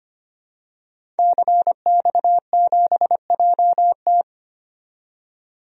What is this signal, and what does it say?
Morse code, heard as a keyed tone, about 25 wpm, 690 Hz. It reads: CX7JT
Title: DXCC-100 Call Signs - 5 Characters - 25wpm